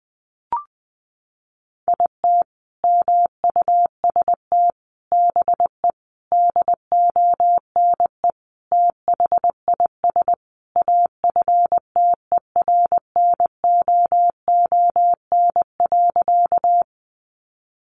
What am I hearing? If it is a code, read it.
Morse code, heard as a keyed tone, about 19 wpm, 700 Hz. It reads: IT MUST BE DONE THIS AFTERNOON.
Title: Sentences from Top 1500 Words 20wpm